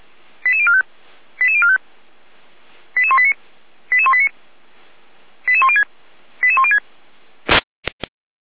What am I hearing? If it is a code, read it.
Five-tone selective call (ZVEI), heard as quick runs of five tones, tones of about 70 ms. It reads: 88045 88045 88189 88189 88106 88106